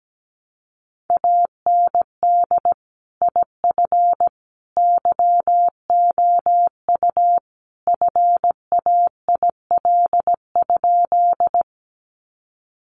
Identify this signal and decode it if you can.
Morse code, heard as a keyed tone, about 17 wpm, 700 Hz. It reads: AND IF YOU FAIL?
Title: Sentences from Top 1100 Words 17wpm